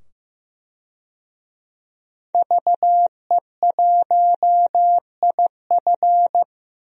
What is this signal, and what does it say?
Morse code, heard as a keyed tone, about 15 wpm, 700 Hz. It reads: VE1IF
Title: DXCC-100 Call Signs - 5 Characters - 15wpm